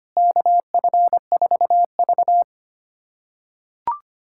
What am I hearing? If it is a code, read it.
Morse code, heard as a keyed tone, about 25 wpm, 700 Hz. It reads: KF4V
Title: USA Callsign DE Callsign - 4 Characters - 25wpm